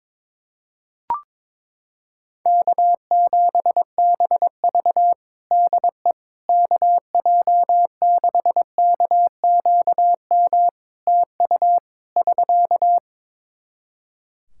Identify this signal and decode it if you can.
Morse code, heard as a keyed tone, about 22 wpm, 700 Hz. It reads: K7BV DE KJ6KQM TU <SK>
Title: Practice QSO - Exchange 4 - Ending - 22wpm